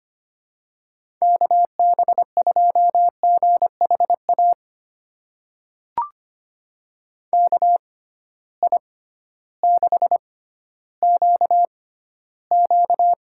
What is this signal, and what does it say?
Morse code, heard as a keyed tone, about 25 wpm, spaced as for 14 wpm, 700 Hz. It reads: KB2GHA KI6QQ
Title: USA Call Signs - 6 Characters - 6x Character Spacing - 25wpm